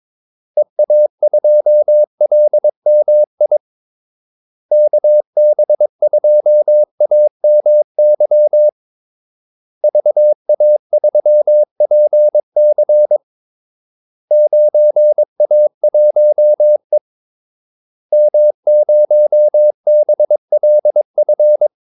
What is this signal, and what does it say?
Morse code, heard as a keyed tone, about 22 wpm, 600 Hz. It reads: EA2LMI KB2AMY VA3PC 9A1E M0BLF
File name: Callsigns-22wpm